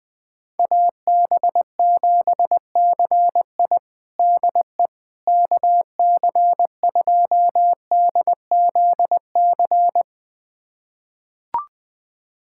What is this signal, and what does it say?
Morse code, heard as a keyed tone, about 20 wpm, 700 Hz. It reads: AB7CI DE KC2DZC
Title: USA Callsign DE Callsign 20wpm